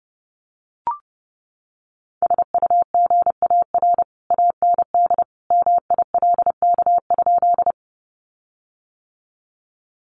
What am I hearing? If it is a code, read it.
Morse code, heard as a keyed tone, about 30 wpm, 700 Hz. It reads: SUGAR AND MILK?
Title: Sentences from Top 1500 Words 30wpm